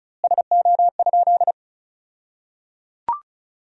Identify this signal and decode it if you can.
Morse code, heard as a keyed tone, about 36 wpm, 700 Hz. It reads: SO?
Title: 2x Word-Spacing - Sentences from Top 400 Words - 35wpm